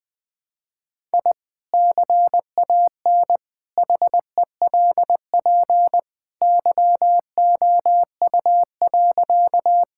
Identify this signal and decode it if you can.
Morse code, heard as a keyed tone, about 20 wpm, 700 Hz. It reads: I CAN HELP YOU.
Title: Sentences with 2-Word Phrases - 20wpm